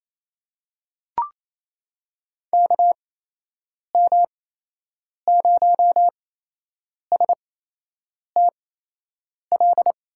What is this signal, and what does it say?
Morse code, heard as a keyed tone, about 27 wpm, spaced as for 8 wpm, 700 Hz. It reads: KM0STL
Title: USA Call Signs - 6 Characters - 8x Character Spacing - 28wpm